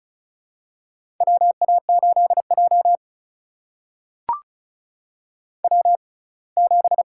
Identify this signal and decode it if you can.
Morse code, heard as a keyed tone, about 36 wpm, spaced as for 24 wpm, 700 Hz. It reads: WA8J W7
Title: USA Call Signs - 4 Characters - 6x Character Spacing - 35wpm